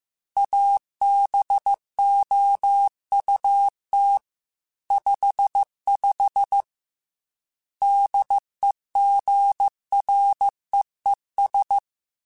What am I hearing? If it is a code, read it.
Morse code, heard as a keyed tone, about 15 wpm, 800 Hz. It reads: ABOUT 55 DEGREES
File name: Weather Sets / Weather Sets-mp3s / Weather 4 24 @13